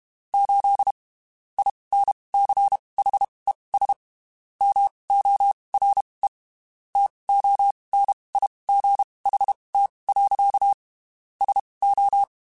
Morse code, heard as a keyed tone, about 32 wpm, spaced as for 23 wpm, 800 Hz. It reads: 8 INCHES MORE TONIGHT. SO
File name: (spaced) / QSOs Spaced-mp3s / WA7QOX 1 @30 sp